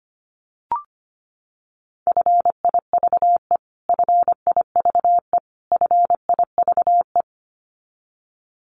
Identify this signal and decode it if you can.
Morse code, heard as a keyed tone, about 25 wpm, 700 Hz. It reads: FIVE FIVE FIVE
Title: Top 500 Words Repeated 25wpm